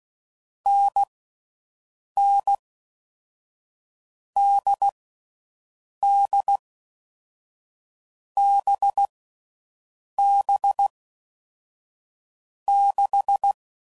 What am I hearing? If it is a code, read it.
Morse code, heard as a keyed tone, about 16 wpm, 800 Hz. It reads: N N D D B B 6
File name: Mixed / Short Runs / NDB NDB6